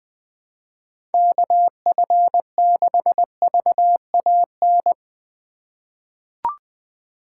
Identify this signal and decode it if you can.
Morse code, heard as a keyed tone, about 20 wpm, 700 Hz. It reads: KF6VAN
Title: USA Call Signs - 6 Characters - 2x Character Spacing - 20wpm